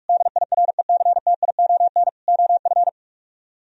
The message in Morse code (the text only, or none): DIRECTION OF